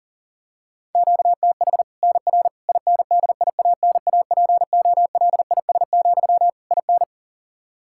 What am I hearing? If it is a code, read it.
Morse code, heard as a keyed tone, about 40 wpm, 700 Hz. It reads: QTH NR INDIANAPOLIS, IN
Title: Practice QSO - Exchange 1 - Beginning - 40wpm